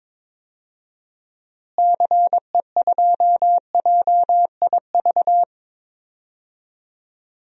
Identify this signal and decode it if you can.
Morse code, heard as a keyed tone, about 21 wpm, 700 Hz. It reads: CE2JIV
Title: DXCC-100 Call Signs - 6 Characters - Repeated 3x - 22wpm